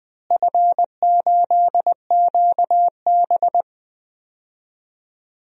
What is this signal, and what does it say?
Morse code, heard as a keyed tone, about 20 wpm, 700 Hz. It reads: F8QB
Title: DXCC-100 Call Signs - 4 Characters - Repeated 3x - 20wpm